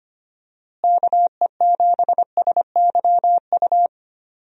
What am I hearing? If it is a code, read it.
Morse code, heard as a keyed tone, about 25 wpm, 700 Hz. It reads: KE7SYU
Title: USA Call Signs - 6 Characters - Repeated 2x - 25wpm